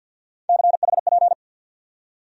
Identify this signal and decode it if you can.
Morse code, heard as a keyed tone, about 53 wpm, 700 Hz. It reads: CHP
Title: USA Call Sign Postfixes - 3 Letters 50wpm